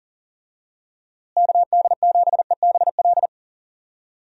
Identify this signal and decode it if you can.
Morse code, heard as a keyed tone, about 40 wpm, 700 Hz. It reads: KD7EBL
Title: USA Call Signs - 6 Characters - Repeated 3x - 40wpm